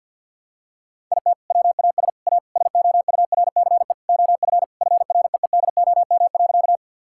Morse code, heard as a keyed tone, about 49 wpm, 700 Hz. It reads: IT WAS A SOURCE OF FREEDOM.
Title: Sentences from Top 1000 Words - Review Part 5 - 50wpm